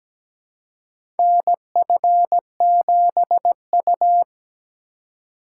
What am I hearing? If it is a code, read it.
Morse code, heard as a keyed tone, about 17 wpm, 700 Hz. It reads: NF7U
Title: USA Call Signs - 4 Characters - 2x Character Spacing - 17wpm